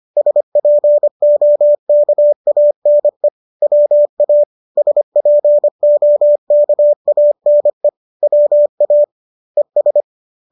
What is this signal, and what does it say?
Morse code, heard as a keyed tone, about 25 wpm, 590 Hz. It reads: SPOKANE WA SPOKANE WA ES